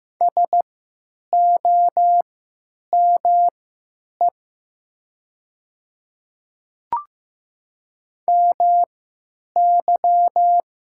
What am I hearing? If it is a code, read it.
Morse code, heard as a keyed tone, about 15 wpm, spaced as for 9 wpm, 700 Hz. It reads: SOME MY
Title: 3x Character Spacing - Top 200 Words - 15wpm